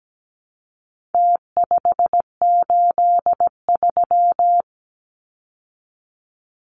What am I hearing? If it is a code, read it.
Morse code, heard as a keyed tone, about 17 wpm, 700 Hz. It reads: T583
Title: Numbers - Contest Style - 4-Digit 17wpm